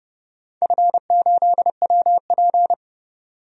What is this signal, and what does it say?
Morse code, heard as a keyed tone, about 31 wpm, 700 Hz. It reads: F8WP
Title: DXCC-100 Call Signs - 4 Characters - 30wpm